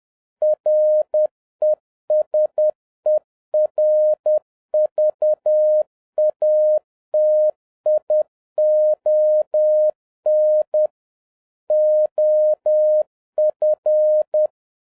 Morse code, heard as a keyed tone, about 10 wpm, 610 Hz. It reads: RESERVATION OF
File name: 2011-07-mp3 / n31 THE 3.11 2 10wpm